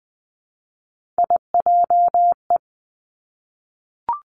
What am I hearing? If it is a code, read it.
Morse code, heard as a keyed tone, about 20 wpm, 700 Hz. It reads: IJE
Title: Three Challenging Letter-Number Combinations 20wpm